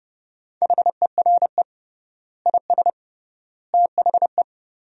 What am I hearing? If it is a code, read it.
Morse code, heard as a keyed tone, about 30 wpm, 700 Hz. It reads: HERE IS THE
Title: 3x Word-Spacing - Sentences from Top 300 Words - 30wpm